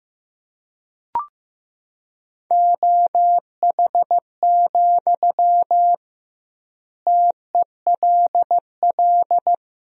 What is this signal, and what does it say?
Morse code, heard as a keyed tone, about 15 wpm, 700 Hz. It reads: OH, TELL
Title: 2x Word-Spacing - Sentences from Top 500 Words - 15wpm